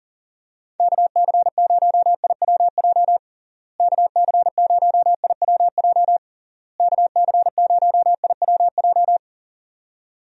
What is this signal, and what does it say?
Morse code, heard as a keyed tone, about 40 wpm, 700 Hz. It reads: KC0IWJ KC0IWJ KC0IWJ
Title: USA Call Signs - 6 Characters - Repeated 3x - 40wpm